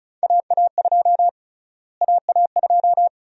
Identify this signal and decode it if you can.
Morse code, heard as a keyed tone, about 36 wpm, 700 Hz. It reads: AA2 AA2
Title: USA Call Sign Prefixes - Repeated 2x - 35wpm